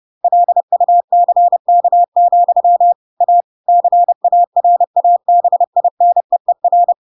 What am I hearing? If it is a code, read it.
Morse code, heard as a keyed tone, about 30 wpm, 700 Hz. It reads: LUCK, A CARABINEER
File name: Pinocchio / Pinocchio ch3 30 30 ik1hge